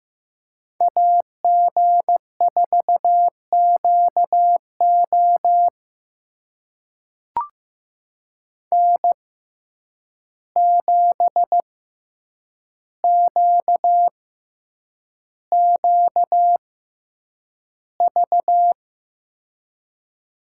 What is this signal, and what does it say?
Morse code, heard as a keyed tone, about 15 wpm, spaced as for 8 wpm, 700 Hz. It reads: AG4QO N7QQV
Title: USA Call Signs - 5 Characters - 6x Character Spacing - 15wpm